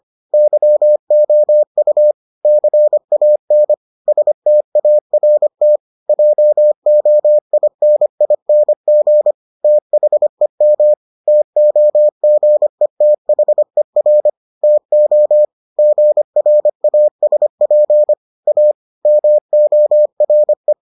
Morse code, heard as a keyed tone, about 25 wpm, 600 Hz. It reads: YOU CAN START JOINING THEM TOGETHER TO GRASP A MORE